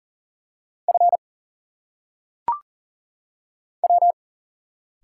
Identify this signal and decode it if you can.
Morse code, heard as a keyed tone, about 38 wpm, 700 Hz. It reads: F W
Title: Single Letters - Phonetic Pronunciation 40wpm